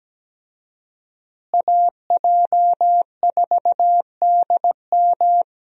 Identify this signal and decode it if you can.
Morse code, heard as a keyed tone, about 17 wpm, 700 Hz. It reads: AJ4DM
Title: USA Call Signs - 5 Characters - Repeated 2x - 17wpm